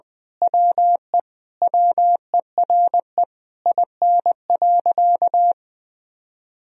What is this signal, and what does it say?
Morse code, heard as a keyed tone, about 20 wpm, 700 Hz. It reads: WE WERE IN.
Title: Sentences from Top 1000 Words - Review Part 3 - 20wpm